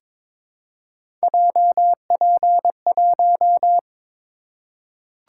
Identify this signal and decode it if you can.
Morse code, heard as a keyed tone, about 22 wpm, 700 Hz. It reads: JP1
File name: practice / 22wpm / Three Letter-Number Combinations - 22wpm